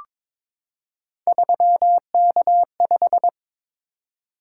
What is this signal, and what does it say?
Morse code, heard as a keyed tone, about 23 wpm, 700 Hz. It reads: 3K5
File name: practice / 22wpm / Three Letter-Number Combinations - 22wpm